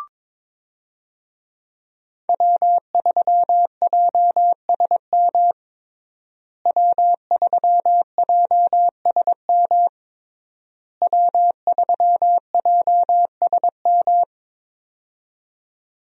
Morse code, heard as a keyed tone, about 22 wpm, 700 Hz. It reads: W3JSM W3JSM W3JSM
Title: USA Call Signs - 5 Characters - Repeated 3x - 22wpm